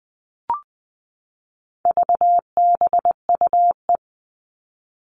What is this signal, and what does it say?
Morse code, heard as a keyed tone, about 20 wpm, 700 Hz. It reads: VBUE
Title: All Cut Numbers - 4-Digit 20wpm